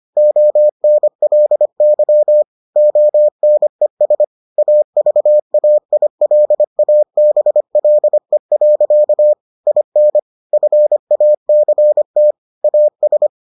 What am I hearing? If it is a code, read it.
Morse code, heard as a keyed tone, about 25 wpm, 600 Hz. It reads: ONLY ONES AVAILABLE. IN FACT AS